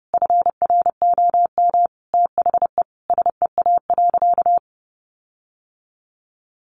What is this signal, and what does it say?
Morse code, heard as a keyed tone, about 30 wpm, 700 Hz. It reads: FROM THE SEA.
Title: Sentences from Top 1400 Words 30wpm